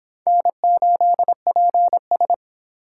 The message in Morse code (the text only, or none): N8PS